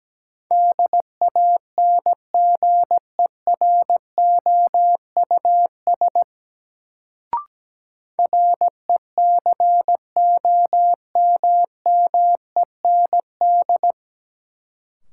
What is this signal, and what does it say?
Morse code, heard as a keyed tone, about 17 wpm, 700 Hz. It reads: DANGEROUS RECOMMEND
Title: Sentences from Top 1200 Words - New Word Drill Alt - 17wpm